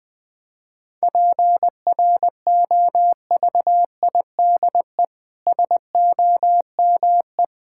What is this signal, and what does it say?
Morse code, heard as a keyed tone, about 20 wpm, 700 Hz. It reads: PROVIDE SOME
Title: Sentences from Top 1000 Words - Review Part 6 - 20wpm